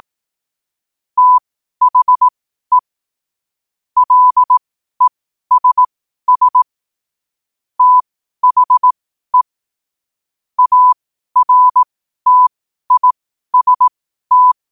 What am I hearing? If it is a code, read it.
Morse code, heard as a keyed tone, about 18 wpm, spaced as for 12 wpm, 1000 Hz. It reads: THE LESS THE ARTIST
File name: qotdfiles / 12wpm